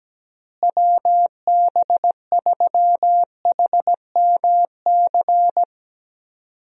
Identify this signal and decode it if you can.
Morse code, heard as a keyed tone, about 17 wpm, 700 Hz. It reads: WB3HMC
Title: USA Call Signs - 6 Characters - Repeated 2x - 17wpm